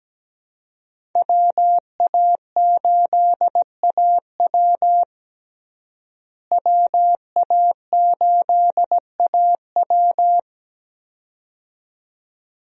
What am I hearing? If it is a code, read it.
Morse code, heard as a keyed tone, about 17 wpm, 700 Hz. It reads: WA8AW WA8AW
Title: USA Call Signs - 5 Characters - Repeated 2x - 17wpm